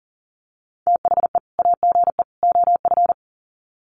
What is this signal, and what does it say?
Morse code, heard as a keyed tone, about 40 wpm, 700 Hz. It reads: THE AGE OF
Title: Sets of 3 Words 40wpm